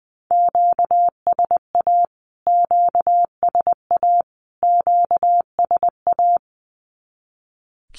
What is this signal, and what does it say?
Morse code, heard as a keyed tone, about 20 wpm, 700 Hz. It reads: QSA QSA QSA
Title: Q-Codes Repeated 3x 20wpm